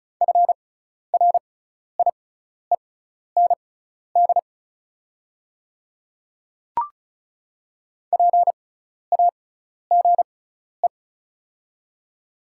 Morse code, heard as a keyed tone, about 36 wpm, spaced as for 12 wpm, 700 Hz. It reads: FRIEND PAGE
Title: 6x Character Spacing - Top 500 Words - 35wpm